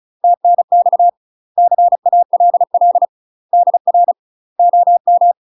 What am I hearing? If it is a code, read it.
Morse code, heard as a keyed tone, about 34 wpm, 700 Hz. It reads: TNX CALL DR OM
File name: UK MASTER QSOs TRAINING / 35WPM / UK MASTER QSOs 0001